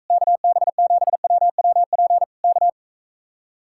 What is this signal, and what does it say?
Morse code, heard as a keyed tone, about 45 wpm, 700 Hz. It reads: KB7WWP K